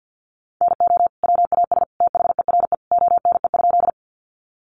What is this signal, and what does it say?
Morse code, heard as a keyed tone, about 50 wpm, 700 Hz. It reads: NO WAS THERE ONE?
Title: Sentences from Top 100 Words 50wpm